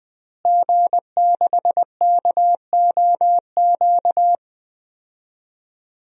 Morse code, heard as a keyed tone, about 21 wpm, 700 Hz. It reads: G6KOQ
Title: DXCC-100 Call Signs - 5 Characters - Repeated 2x - 20wpm